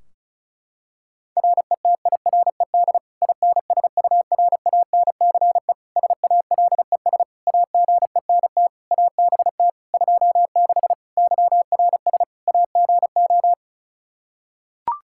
Morse code, heard as a keyed tone, about 34 wpm, 700 Hz. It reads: RETIRED INSURANCE SALES AGENT ABT 26 YRS AGO
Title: Practice QSO - Exchange 3 - More Info - 35wpm